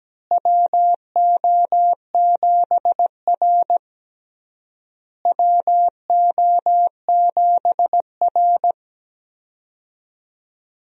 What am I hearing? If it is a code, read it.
Morse code, heard as a keyed tone, about 17 wpm, 700 Hz. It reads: WO7R WO7R
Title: USA Call Signs - 4 Characters - Repeated 3x - 17wpm